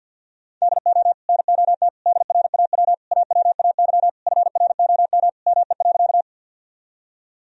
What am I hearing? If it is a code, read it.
Morse code, heard as a keyed tone, about 48 wpm, 700 Hz. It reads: DO NOT DRAW AWAY FROM ME.
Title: Sentences from Top 1000 Words - Review Part 1 - 50wpm